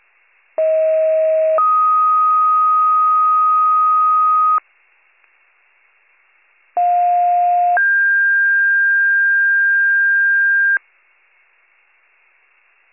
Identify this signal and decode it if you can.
Two-tone page, heard as a one-second tone, then a three-second tone, about 640 then 1250 Hz, about 710 then 1600 Hz. It is heard from 0.6 to 10.8 s.